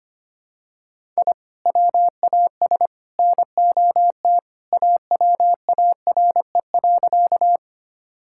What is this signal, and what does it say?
Morse code, heard as a keyed tone, about 25 wpm, 700 Hz. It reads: I WAS NOT AWARE.
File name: practice / 25wpm / Sentences with 3-Word Phrases - 25wpm